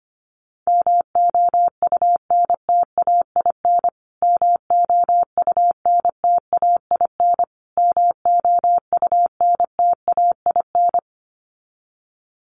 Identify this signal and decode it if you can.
Morse code, heard as a keyed tone, about 25 wpm, 700 Hz. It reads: MOUNTAIN MOUNTAIN MOUNTAIN
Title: Top 300 Words Repeated 25wpm